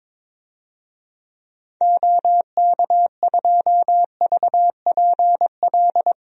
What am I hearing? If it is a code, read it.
Morse code, heard as a keyed tone, about 21 wpm, 700 Hz. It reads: OK2VPL
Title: DXCC-100 Call Signs - 6 Characters - Repeated 2x - 22wpm